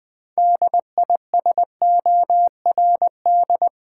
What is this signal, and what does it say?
Morse code, heard as a keyed tone, about 20 wpm, 700 Hz. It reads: DISORD
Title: Build A Word - 3 Letters At A Time - 8-Letter Words - 20wpm